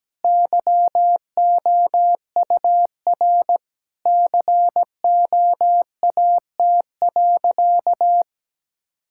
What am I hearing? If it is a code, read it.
Morse code, heard as a keyed tone, about 17 wpm, 700 Hz. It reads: YOUR COAT.
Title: Sentences from Top 1600 Words 17wpm